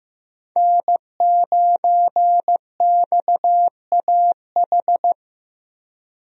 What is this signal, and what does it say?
Morse code, heard as a keyed tone, about 16 wpm, 700 Hz. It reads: N9XAH
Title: USA Call Signs - 5 Characters - Repeated 2x - 15wpm